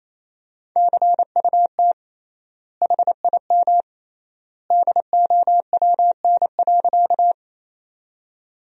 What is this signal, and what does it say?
Morse code, heard as a keyed tone, about 28 wpm, 700 Hz. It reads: CUT HIM DOWN.
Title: 3x Word-Spacing - Sentences from Top 300 Words - 28wpm